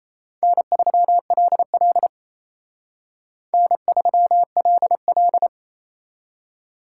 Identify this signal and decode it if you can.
Morse code, heard as a keyed tone, about 35 wpm, 700 Hz. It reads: N3LL N3LL
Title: Speed Racing - USA 1x2 Call Signs 28wpm